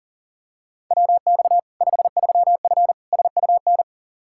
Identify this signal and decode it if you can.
Morse code, heard as a keyed tone, about 40 wpm, 700 Hz. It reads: WX 53F SUN
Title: Practice QSO - Exchange 2 - Info - 40wpm